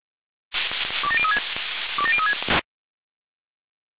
Five-tone selective call (ZVEI), heard as quick runs of five tones, tones of about 70 ms. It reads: 27036 27036